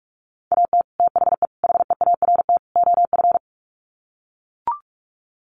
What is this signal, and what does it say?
Morse code, heard as a keyed tone, about 44 wpm, 700 Hz. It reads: AT THE HEART OF
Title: Sets of 4 Words 45wpm